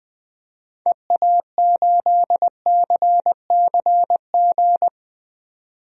Morse code, heard as a keyed tone, about 20 wpm, 700 Hz. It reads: EA8CCG
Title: DXCC-100 Call Signs - 6 Characters - 20wpm